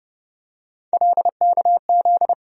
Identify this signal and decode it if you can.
Morse code, heard as a keyed tone, about 30 wpm, 700 Hz. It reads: LKZ
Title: USA Call Sign Postfixes - 3 Letters 30wpm